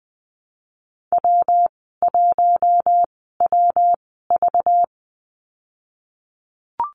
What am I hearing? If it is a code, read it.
Morse code, heard as a keyed tone, about 20 wpm, spaced as for 15 wpm, 700 Hz. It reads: W1WV
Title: 200 Lesson 32g - Letter k - Call Signs 20wpm